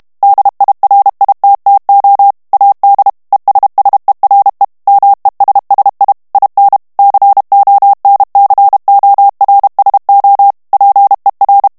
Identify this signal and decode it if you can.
Morse code, heard as a keyed tone, about 32 wpm, 800 Hz. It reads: DIRITTO AD ESSERE MESSI IN CONCORSO PER
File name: ITALIEN texte clair 2 vitesse 30 mots minute